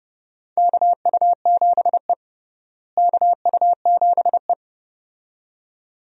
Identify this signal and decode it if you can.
Morse code, heard as a keyed tone, about 30 wpm, 700 Hz. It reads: KU7E KU7E
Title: USA Call Signs - 4 Characters - Repeated 2x - 30wpm